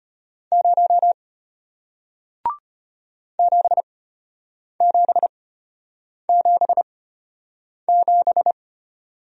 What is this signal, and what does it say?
Morse code, heard as a keyed tone, about 33 wpm, spaced as for 8 wpm, 700 Hz. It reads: 0 7777